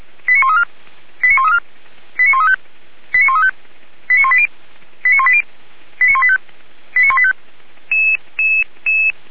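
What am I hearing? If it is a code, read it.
Five-tone selective call (ZVEI), heard as quick runs of five tones, tones of about 70 ms. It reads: 89134 89134 89135 89135 89180 89180 89185 89185
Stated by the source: ZVEI five-tone selective call received by SDR, from the Feuerwehr Orlamünde fire brigade audio server.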